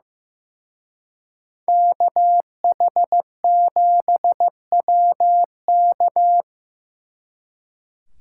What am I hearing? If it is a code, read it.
Morse code, heard as a keyed tone, about 18 wpm, 700 Hz. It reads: KH7WK
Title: Speed Racing - USA Call Signs 15wpm